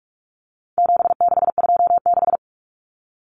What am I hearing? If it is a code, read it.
Morse code, heard as a keyed tone, about 44 wpm, 700 Hz. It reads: 7626
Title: Numbers - 4-Digit 45wpm